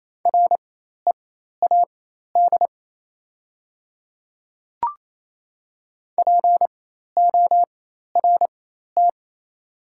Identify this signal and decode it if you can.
Morse code, heard as a keyed tone, about 28 wpm, spaced as for 13 wpm, 690 Hz. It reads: READ PORT
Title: 4x Character Spacing - Top 200 Words - 28wpm